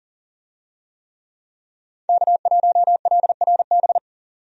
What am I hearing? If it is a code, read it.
Morse code, heard as a keyed tone, about 40 wpm, 700 Hz. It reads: K1LRB
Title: USA Call Signs - 5 Characters - 40wpm